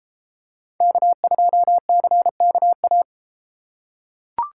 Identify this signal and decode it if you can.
Morse code, heard as a keyed tone, about 30 wpm, 700 Hz. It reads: K2CKA